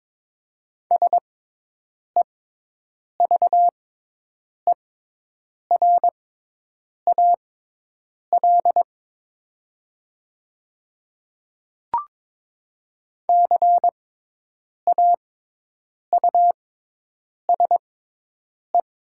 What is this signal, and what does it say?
Morse code, heard as a keyed tone, about 22 wpm, spaced as for 8 wpm, 700 Hz. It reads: SEVERAL CAUSE
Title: 6x Character Spacing - Top 500 Words - 22wpm